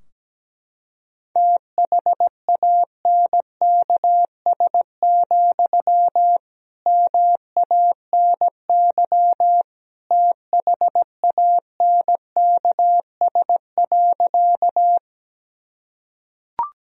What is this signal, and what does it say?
Morse code, heard as a keyed tone, about 17 wpm, 700 Hz. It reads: THANKS, MANY THANKS.
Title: Sentences from Top 1000 Words - Review Part 3 - 17wpm